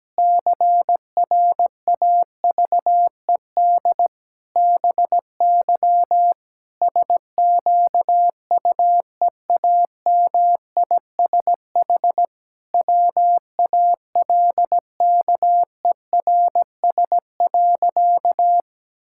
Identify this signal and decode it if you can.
Morse code, heard as a keyed tone, about 17 wpm, 700 Hz. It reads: CRAVED BY SQUEAMISH WALKERS.
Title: Pangram Challenge v2 - 17wpm